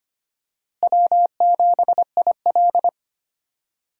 Morse code, heard as a keyed tone, about 25 wpm, 700 Hz. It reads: W7IL
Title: USA Call Signs - 4 Characters - Repeated 2x - 25wpm